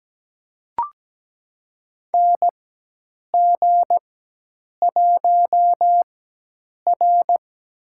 Morse code, heard as a keyed tone, about 17 wpm, spaced as for 8 wpm, 700 Hz. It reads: NG1R
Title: USA Call Signs - 4x Character Spacing - 17wpm